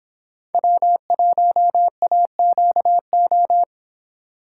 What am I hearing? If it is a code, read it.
Morse code, heard as a keyed tone, about 24 wpm, 700 Hz. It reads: W1AQO